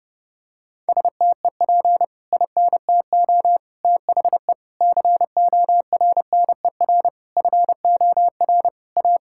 Morse code, heard as a keyed tone, about 32 wpm, 700 Hz. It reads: STEP INTO THE CORNER FOR A
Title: Sentences from Top 1600 Words 30wpm